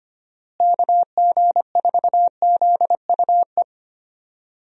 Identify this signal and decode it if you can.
Morse code, heard as a keyed tone, about 25 wpm, 700 Hz. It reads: KG4ZUE
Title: USA Call Signs - 6 Characters - Repeated 2x - 25wpm